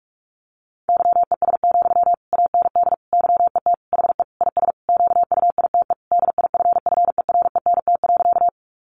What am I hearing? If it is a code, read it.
Morse code, heard as a keyed tone, about 44 wpm, 700 Hz. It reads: YES, AND YET HE IS QUITE DIFFERENT.
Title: Sentences from Top 700 Words 45wpm